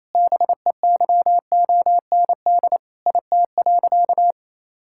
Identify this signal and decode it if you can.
Morse code, heard as a keyed tone, about 28 wpm, 700 Hz. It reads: BEYOND IT.
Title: Sentences from Top 1000 Words - Review Part 6 - 28wpm